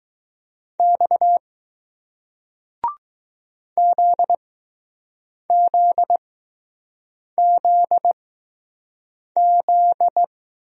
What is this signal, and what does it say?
Morse code, heard as a keyed tone, about 20 wpm, 700 Hz. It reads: X Z Z Z Z